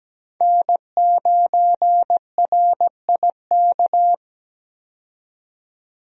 Morse code, heard as a keyed tone, about 17 wpm, 700 Hz. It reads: N9RIK
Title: USA Call Signs - 5 Characters - 17wpm